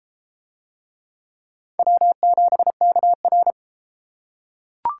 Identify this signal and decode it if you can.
Morse code, heard as a keyed tone, about 31 wpm, 700 Hz. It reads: W7KR